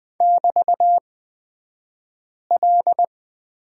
Morse code, heard as a keyed tone, about 20 wpm, spaced as for 5 wpm, 700 Hz. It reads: <BT>L
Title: 223 Lesson 36d - BT - Two Characters 20wpm